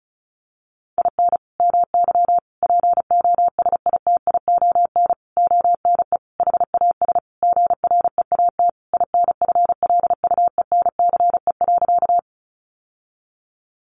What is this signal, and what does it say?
Morse code, heard as a keyed tone, about 34 wpm, 700 Hz. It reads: IN MY POSITION ONE HAS GREAT INFLUENCE.
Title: Sentences from Top 1100 Words 35wpm